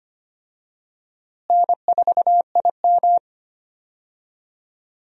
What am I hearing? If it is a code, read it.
Morse code, heard as a keyed tone, about 25 wpm, 700 Hz. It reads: N4IM
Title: USA Call Signs - 4 Characters - Repeated 2x - 25wpm